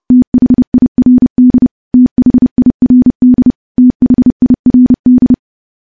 Morse code, heard as a keyed tone, about 31 wpm, 250 Hz. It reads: THIRD THIRD THIRD
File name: morse / morse-training-30WPM